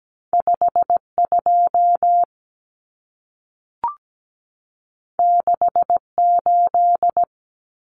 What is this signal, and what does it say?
Morse code, heard as a keyed tone, about 17 wpm, 700 Hz. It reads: 5 2 6 8
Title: Common Cut Numbers - 2-Digit 17wpm